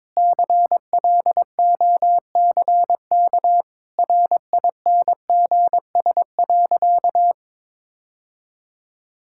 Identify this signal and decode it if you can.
Morse code, heard as a keyed tone, about 22 wpm, 700 Hz. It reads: CLOCK RINGS.
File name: practice / 22wpm / Easy Sentences with Hints - 22wpm